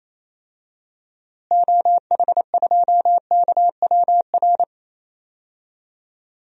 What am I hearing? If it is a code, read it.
Morse code, heard as a keyed tone, about 28 wpm, 700 Hz. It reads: OH2KWR
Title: DXCC-100 Call Signs - 6 Characters - Repeated 2x - 28wpm